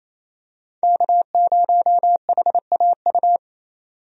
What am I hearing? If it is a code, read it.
Morse code, heard as a keyed tone, about 28 wpm, 700 Hz. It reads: K0HAU
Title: USA Call Signs - 5 Characters - 28wpm